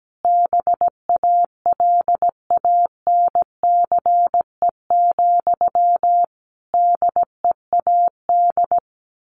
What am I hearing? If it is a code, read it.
Morse code, heard as a keyed tone, about 17 wpm, 700 Hz. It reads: BALANCE, DEAD
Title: A Sound Of Thunder 17wpm